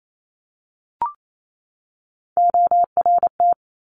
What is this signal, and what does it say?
Morse code, heard as a keyed tone, about 28 wpm, 700 Hz. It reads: ORT
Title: Top Three Letter Combinations in Words 28wpm